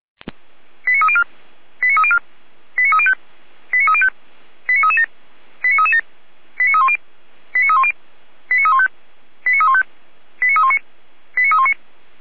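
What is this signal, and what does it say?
Five-tone selective call (ZVEI), heard as quick runs of five tones, tones of about 80 ms. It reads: 89304 89304 89305 89305 89337 89337 89310 89310 89315 89315 89319 89319
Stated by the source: Five-tone ZVEI selective call received on SDR.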